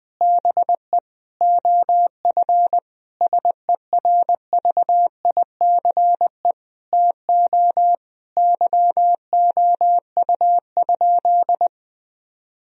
Morse code, heard as a keyed tone, about 20 wpm, 700 Hz. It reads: BE OF SERVICE TO YOU?
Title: Sentences from Top 1000 Words - Review Part 4 - 20wpm